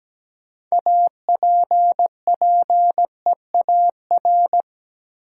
Morse code, heard as a keyed tone, about 17 wpm, 700 Hz. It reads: APPEAR
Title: Sentences from Top 1000 Words - Review Part 2 - New Word Drill - 17wpm